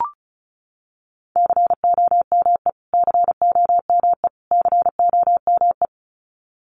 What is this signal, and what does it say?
Morse code, heard as a keyed tone, about 34 wpm, 700 Hz. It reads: COME COME COME
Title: Top 500 Words Repeated 35wpm